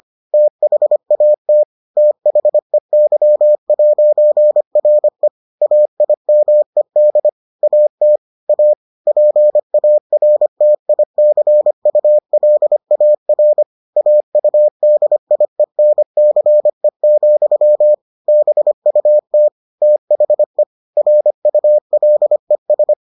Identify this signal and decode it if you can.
Morse code, heard as a keyed tone, about 25 wpm, 600 Hz. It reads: THAT THEY'RE AIMED AT A PARTICULAR AUDIENCE, BUT THE RULES